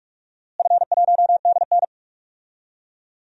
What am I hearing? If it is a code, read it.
Morse code, heard as a keyed tone, about 46 wpm, 700 Hz. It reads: F1DN
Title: DXCC-100 Call Signs - 4 Characters - Repeated 3x - 45wpm